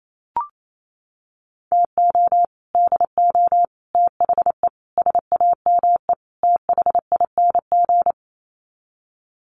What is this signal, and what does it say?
Morse code, heard as a keyed tone, about 28 wpm, 700 Hz. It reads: TO DO THE SAME THING
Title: Sets of 5 Words Encore 28wpm